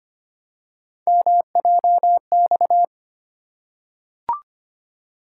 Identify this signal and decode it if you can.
Morse code, heard as a keyed tone, about 25 wpm, 700 Hz. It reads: MJX
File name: Three Letter-Number Combinations - 25wpm